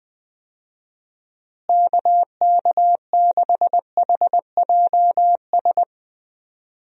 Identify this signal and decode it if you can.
Morse code, heard as a keyed tone, about 20 wpm, 700 Hz. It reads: KK6HJS
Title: USA Call Signs - 6 Characters - Repeated 2x - 20wpm